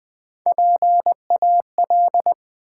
Morse code, heard as a keyed tone, about 20 wpm, 700 Hz. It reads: PAL
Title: Build A Word - 1 Letter At A Time - 4-Letter Words - 20wpm